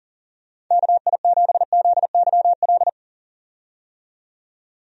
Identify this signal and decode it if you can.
Morse code, heard as a keyed tone, about 38 wpm, 700 Hz. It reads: KI7ZYL
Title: USA Call Signs - 6 Characters - 40wpm